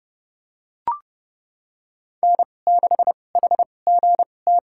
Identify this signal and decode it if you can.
Morse code, heard as a keyed tone, about 32 wpm, spaced as for 22 wpm, 700 Hz. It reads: N6HGT
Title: USA Call Signs - 5 Characters - 2x Character Spacing - 30wpm